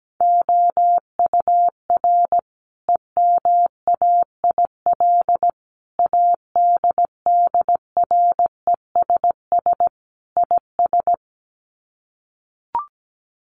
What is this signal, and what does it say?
Morse code, heard as a keyed tone, about 17 wpm, 700 Hz. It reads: OUR EMAIL ADDRESS IS
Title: Sets of 5 Words Encore2 17wpm